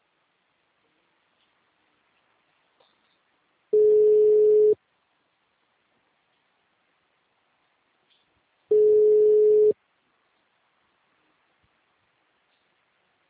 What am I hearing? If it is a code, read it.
Ringback tone (Europe, 425 Hz).